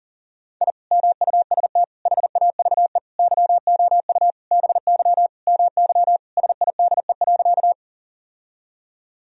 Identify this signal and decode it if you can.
Morse code, heard as a keyed tone, about 40 wpm, 700 Hz. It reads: I MUST HAVE YOU BY MY SIDE.
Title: Sentences from Top 1000 Words - Review Part 1 - 40wpm